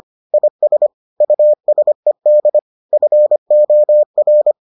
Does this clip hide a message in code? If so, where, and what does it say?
Yes: Morse code from 0.3 to 4.5 s, heard as a keyed tone, about 25 wpm, 600 Hz. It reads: IS USED FOR